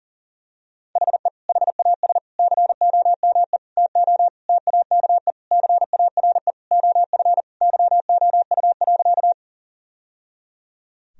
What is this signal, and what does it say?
Morse code, heard as a keyed tone, about 40 wpm, 700 Hz. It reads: HE HAS COME TO TAKE CARE OF YOU.
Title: Sentences from Top 1000 Words - Review Part 1 - 40wpm